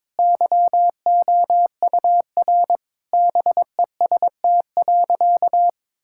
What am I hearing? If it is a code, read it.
Morse code, heard as a keyed tone, about 22 wpm, 700 Hz. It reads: YOUR BEST.
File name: Sentences with 7-Word Phrases - 22wpm